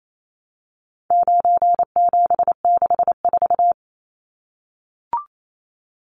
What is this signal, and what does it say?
Morse code, heard as a keyed tone, about 28 wpm, 700 Hz. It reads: 9764
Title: Numbers - 4-Digit 28wpm